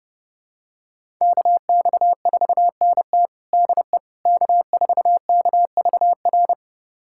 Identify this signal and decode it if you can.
Morse code, heard as a keyed tone, about 32 wpm, 700 Hz. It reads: KX4NT DE K4KVR
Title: USA Callsign DE Callsign - 5 Characters - 30wpm